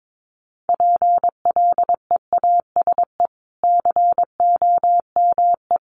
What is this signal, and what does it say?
Morse code, heard as a keyed tone, about 22 wpm, 700 Hz. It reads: PLEASE COME
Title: Sentences from Top 1500 Words 22wpm